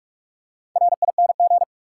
Morse code, heard as a keyed tone, about 47 wpm, 700 Hz. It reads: RING
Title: Four Letter Words - 45wpm